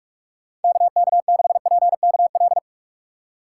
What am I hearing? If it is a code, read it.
Morse code, heard as a keyed tone, about 46 wpm, 700 Hz. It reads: KK6PKL
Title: USA Call Signs - 6 Characters - 4x Character Spacing - 45wpm